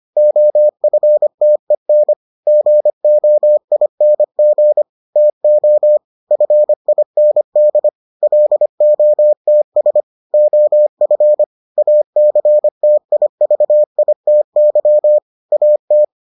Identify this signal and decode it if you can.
Morse code, heard as a keyed tone, about 25 wpm, 600 Hz. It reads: OFTEN GOING TO FIND LOTS OF ACTIVITY AT